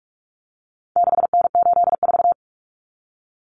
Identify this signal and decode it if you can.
Morse code, heard as a keyed tone, about 44 wpm, 700 Hz. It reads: 6N84
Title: Common Cut Numbers - 4-Digit 45wpm